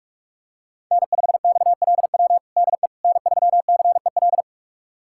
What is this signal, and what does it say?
Morse code, heard as a keyed tone, about 46 wpm, 700 Hz. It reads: N5XLW DE N3CEL
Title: USA Callsign DE Callsign 45wpm